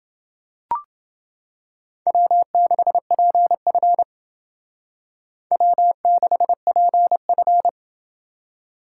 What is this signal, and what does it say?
Morse code, heard as a keyed tone, about 28 wpm, 700 Hz. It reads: W6PF W6PF